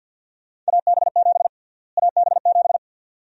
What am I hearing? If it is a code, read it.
Morse code, heard as a keyed tone, about 52 wpm, 700 Hz. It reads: AB7 AB7
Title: USA Call Sign Prefixes - Repeated 2x - 50wpm